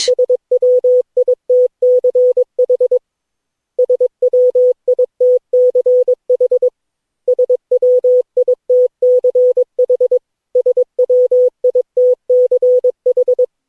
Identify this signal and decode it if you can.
Morse code, heard as a keyed tone, about 22 wpm, 500 Hz. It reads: SWITCH SWITCH SWITCH SWITCH
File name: cw / Session9 cw training